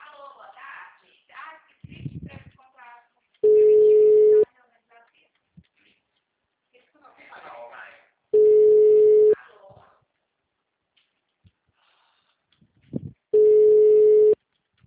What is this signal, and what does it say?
Ringback tone (Europe, 425 Hz), with speech.